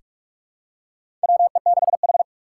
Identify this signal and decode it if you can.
Morse code, heard as a keyed tone, about 46 wpm, 700 Hz. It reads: WE6H
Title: USA Call Signs - 4 Characters - 8x Character Spacing - 45wpm